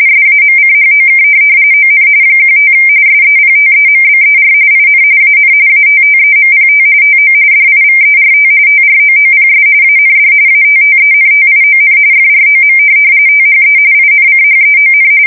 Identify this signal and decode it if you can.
RTTY, about 45.45 baud. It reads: SAMCHOK ON 15 SEPTEMBER IN AN ATTEMPT TO DIVERT TROOPS AND ATTENTION FROM THE INCHEON
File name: fsk170 / USS-Missouri